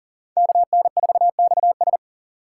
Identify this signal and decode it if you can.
Morse code, heard as a keyed tone, about 38 wpm, 700 Hz. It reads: KN4XS
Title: USA Call Signs - 5 Characters - 6x Character Spacing - 40wpm